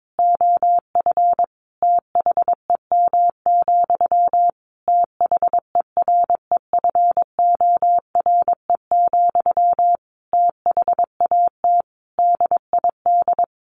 Morse code, heard as a keyed tone, about 22 wpm, 700 Hz. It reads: OF THEM, THEREFORE, THAT DID
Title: Sentences from Top 800 Words 22wpm